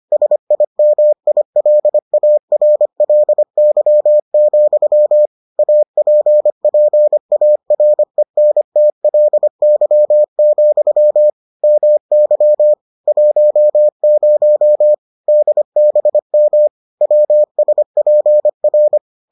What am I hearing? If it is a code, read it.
Morse code, heard as a keyed tone, about 25 wpm, 600 Hz. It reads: SIMILARLY, APPARENTLY, MY 10 DBM WSPR